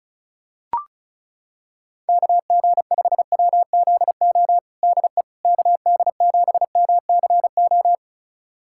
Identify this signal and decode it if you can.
Morse code, heard as a keyed tone, about 34 wpm, 700 Hz. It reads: KG5WZO DE KD7MCO
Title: USA Callsign DE Callsign 35wpm